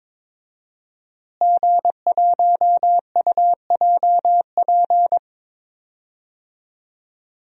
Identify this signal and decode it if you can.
Morse code, heard as a keyed tone, about 22 wpm, 700 Hz. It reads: G1UJP
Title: DXCC-100 Call Signs - 5 Characters - Repeated 3x - 22wpm